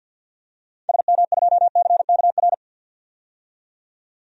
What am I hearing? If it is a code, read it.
Morse code, heard as a keyed tone, about 53 wpm, 700 Hz. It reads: SM2CKR
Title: DXCC-100 Call Signs - 6 Characters - 50wpm